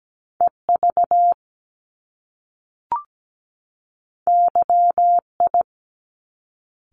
Morse code, heard as a keyed tone, about 17 wpm, 700 Hz. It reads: EV YI
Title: Top Two Letter Combinations in Words 17wpm